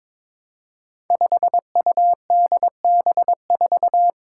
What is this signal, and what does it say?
Morse code, heard as a keyed tone, about 22 wpm, 700 Hz. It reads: 5UDB4
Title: Five Challenging Letter-Number Combinations 22wpm